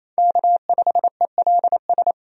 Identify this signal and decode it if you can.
Morse code, heard as a keyed tone, about 27 wpm, 700 Hz. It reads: K5ELS
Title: USA Call Signs - 5 Characters - 28wpm